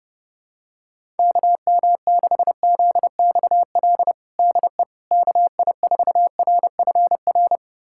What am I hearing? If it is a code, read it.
Morse code, heard as a keyed tone, about 30 wpm, 700 Hz. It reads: KM6ZXL DE KI4RFR
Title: USA Callsign DE Callsign 30wpm